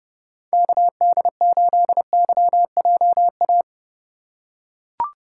Morse code, heard as a keyed tone, about 32 wpm, 700 Hz. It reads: KD8YJA
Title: USA Call Signs - 6 Characters - 3x Character Spacing - 30wpm